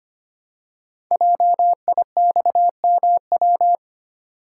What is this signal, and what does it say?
Morse code, heard as a keyed tone, about 25 wpm, 700 Hz. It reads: JIXMW
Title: Code Groups - 5 Letters - 25wpm